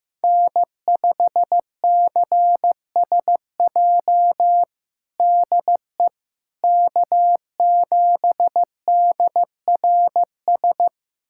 Morse code, heard as a keyed tone, about 15 wpm, 700 Hz. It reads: N5CSJ DE K7DRS
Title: USA Callsign DE Callsign - 5 Characters - 15wpm